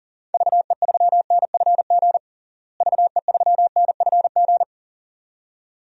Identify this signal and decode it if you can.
Morse code, heard as a keyed tone, about 40 wpm, 700 Hz. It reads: VE3NFG VE3NFG
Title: DXCC-100 Call Signs - 6 Characters - Repeated 2x - 40wpm